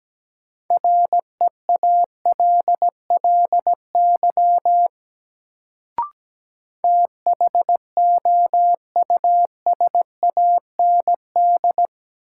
Morse code, heard as a keyed tone, about 17 wpm, 700 Hz. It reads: REALLY THOUSAND
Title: Sentences from Top 600 Words - New Word Drill - 17wpm